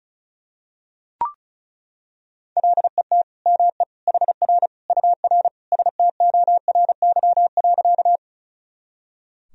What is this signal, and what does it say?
Morse code, heard as a keyed tone, about 34 wpm, 700 Hz. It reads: LET ME HR UR STORY.
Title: Sentences from Top 300 Words and Abbreviations 35wpm